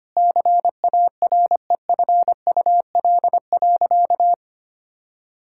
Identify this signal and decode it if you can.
Morse code, heard as a keyed tone, about 25 wpm, 700 Hz. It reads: CAREFUL.
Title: Sentences from Top 1600 Words 25wpm